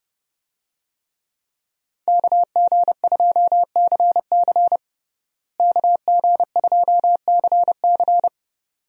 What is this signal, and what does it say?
Morse code, heard as a keyed tone, about 30 wpm, 700 Hz. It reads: KG2CC KG2CC
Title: USA Call Signs - 5 Characters - Repeated 2x - 30wpm